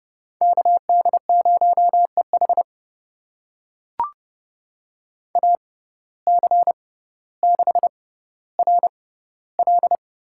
Morse code, heard as a keyed tone, about 31 wpm, spaced as for 16 wpm, 700 Hz. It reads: KD0EH AC6RL
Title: USA Call Signs - 5 Characters - 6x Character Spacing - 30wpm